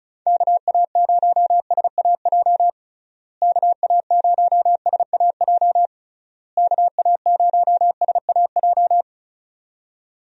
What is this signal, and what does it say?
Morse code, heard as a keyed tone, about 36 wpm, 700 Hz. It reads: KA0SAJ KA0SAJ KA0SAJ
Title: USA Call Signs - 6 Characters - Repeated 3x - 35wpm